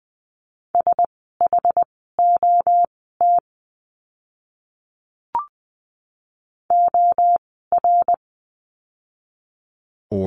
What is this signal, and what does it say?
Morse code, heard as a keyed tone, about 20 wpm, spaced as for 15 wpm, 700 Hz. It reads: SHOT OR
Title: 061 Lesson 11f - Letter h - Words 20wpm